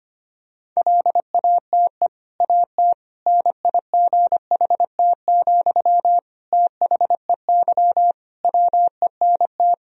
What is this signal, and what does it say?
Morse code, heard as a keyed tone, about 25 wpm, 700 Hz. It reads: LATE AT NIGHT, THEY WENT
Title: Sentences from Top 1000 Words - Review Part 3 - 25wpm